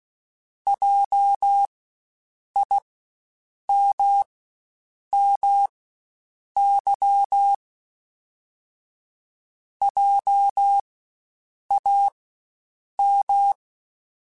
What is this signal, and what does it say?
Morse code, heard as a keyed tone, about 16 wpm, spaced as for 7 wpm, 800 Hz. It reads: JIMMY JAM
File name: Spaced Alphas / Spaced Alphas mp3s / Js sp @15